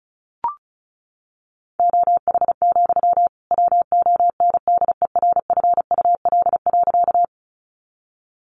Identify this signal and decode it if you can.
Morse code, heard as a keyed tone, about 34 wpm, 700 Hz. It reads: OH, WONDERFUL.
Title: Sentences from Top 1100 Words 35wpm